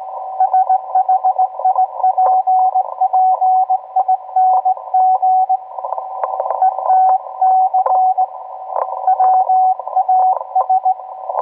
Morse code, heard as a keyed tone, about 18 wpm, 750 Hz, with radio noise. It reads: SHIPPING ETC WAS